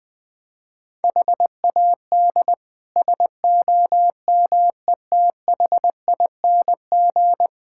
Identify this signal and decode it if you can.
Morse code, heard as a keyed tone, about 20 wpm, 700 Hz. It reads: HAD SOMETHING
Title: Build A Phrase - 5-Word Phrases - 20wpm